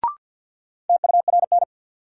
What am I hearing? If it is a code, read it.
Morse code, heard as a keyed tone, about 50 wpm, 700 Hz. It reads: TURN
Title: Sentences from Top 200 Words - New Word Drill - 50wpm